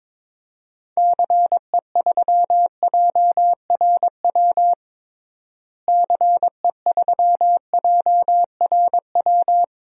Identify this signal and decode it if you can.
Morse code, heard as a keyed tone, about 22 wpm, 700 Hz. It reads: CE3JRW CE3JRW
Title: DXCC-100 Call Signs - 6 Characters - Repeated 2x - 22wpm